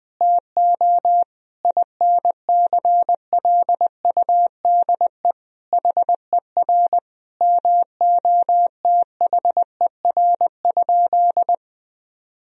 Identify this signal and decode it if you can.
Morse code, heard as a keyed tone, about 20 wpm, 700 Hz. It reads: TO INCLUDE HER MOTHER?
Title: Sentences from Top 1000 Words - Review Part 3 - 20wpm